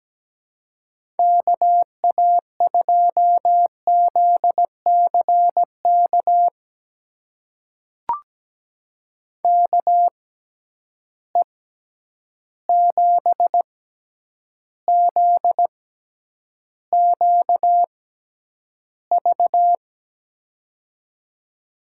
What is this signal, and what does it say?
Morse code, heard as a keyed tone, about 17 wpm, spaced as for 9 wpm, 700 Hz. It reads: KA2ZCK KE7ZQV
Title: USA Call Signs - 6 Characters - 6x Character Spacing - 17wpm